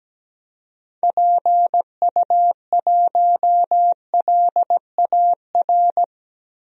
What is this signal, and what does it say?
Morse code, heard as a keyed tone, about 17 wpm, 700 Hz. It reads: PU1LAR
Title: DXCC-100 Call Signs - 6 Characters - Repeated 3x - 17wpm